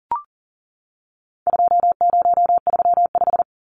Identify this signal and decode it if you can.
Morse code, heard as a keyed tone, about 40 wpm, 700 Hz. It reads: 2035
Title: Numbers - 4-Digit 40wpm